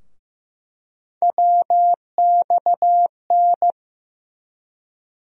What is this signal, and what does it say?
Morse code, heard as a keyed tone, about 15 wpm, 700 Hz. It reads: WXN
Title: USA Call Sign Postfixes - 3 Letters 15wpm